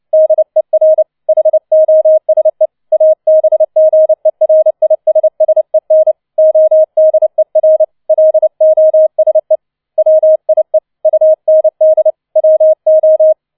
Morse code, heard as a keyed tone, about 29 wpm, 630 Hz. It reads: DER HOSE ABGERISSEN ODER LOSE WIE UND WO